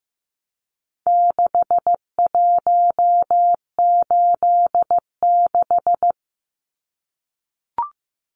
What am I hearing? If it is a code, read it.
Morse code, heard as a keyed tone, about 15 wpm, 700 Hz. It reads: 6186
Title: Numbers - 4-Digit 15wpm